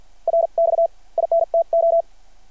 Morse code, heard as a keyed tone, about 52 wpm, spaced as for 44 wpm, 660 Hz. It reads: RX INTO
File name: archived / CRSnet